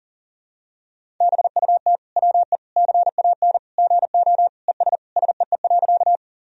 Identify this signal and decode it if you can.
Morse code, heard as a keyed tone, about 40 wpm, 700 Hz. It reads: BUT WE CAN GO ES SEE.
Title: Sentences from Top 100 Words and Abbreviations 40wpm